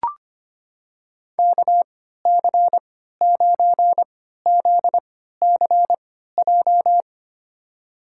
Morse code, heard as a keyed tone, about 25 wpm, spaced as for 14 wpm, 700 Hz. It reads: KC9ZCJ
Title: USA Call Signs - 3x Character Spacing - 25wpm